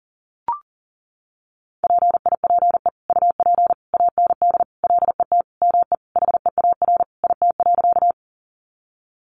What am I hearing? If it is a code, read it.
Morse code, heard as a keyed tone, about 40 wpm, 700 Hz. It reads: PIPE UP AND LET ME HEAR IT.
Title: Sentences from Top 1500 Words 40wpm